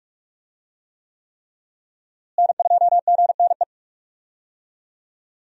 Morse code, heard as a keyed tone, about 43 wpm, 700 Hz. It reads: N2GNE
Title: USA Call Signs - 5 Characters - Repeated 3x - 45wpm